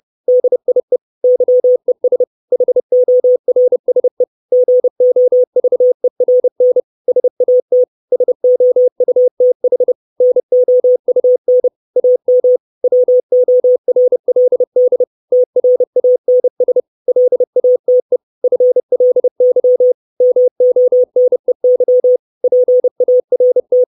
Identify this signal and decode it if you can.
Morse code, heard as a keyed tone, about 30 wpm, 500 Hz. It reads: DIE YES HORSE GOVERN SAT SOUTH NOUN AM WORLD TRANS LATE FLY MONEY PART
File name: 632 Words 30WPM